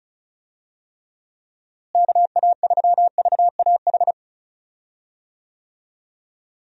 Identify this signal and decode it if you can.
Morse code, heard as a keyed tone, about 34 wpm, 700 Hz. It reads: KA3VAH
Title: USA Call Signs - 6 Characters - Repeated 2x - 35wpm